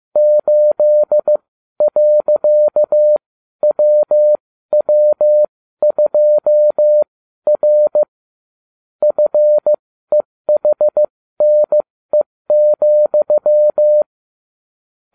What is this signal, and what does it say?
Morse code, heard as a keyed tone, about 15 wpm, spaced as for 12 wpm, 600 Hz. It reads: 8.WW2R FEHNE,